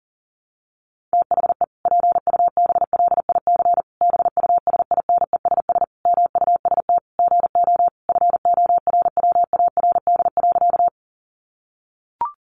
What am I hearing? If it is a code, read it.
Morse code, heard as a keyed tone, about 40 wpm, 700 Hz. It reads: THE PUBLIC BUSINESS MUST GO FORWARD.
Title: Sentences from Top 1300 Words 40wpm